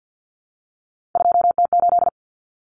Morse code, heard as a keyed tone, about 48 wpm, 700 Hz. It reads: 2T8
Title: Common Cut Numbers - 3-Digit 50wpm